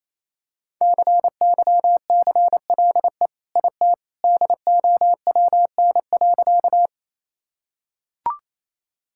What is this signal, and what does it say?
Morse code, heard as a keyed tone, about 28 wpm, 700 Hz. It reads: CYCLE IT DOWN.
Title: Sentences from Top 1000 Words - Review Part 6 - 28wpm